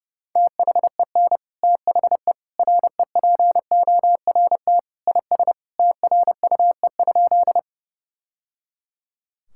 Morse code, heard as a keyed tone, about 32 wpm, 700 Hz. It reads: THEN THE REPORT IS TRUE?
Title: Sentences from Top 1000 Words - Review Part 6 - 30wpm